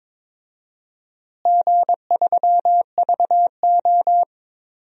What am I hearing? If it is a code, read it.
Morse code, heard as a keyed tone, about 22 wpm, 700 Hz. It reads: G3VO
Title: DXCC-100 Call Signs - 4 Characters - Repeated 3x - 22wpm